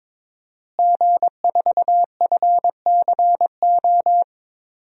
Morse code, heard as a keyed tone, about 21 wpm, 700 Hz. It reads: G4FCO
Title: DXCC-100 Call Signs - 5 Characters - Repeated 3x - 22wpm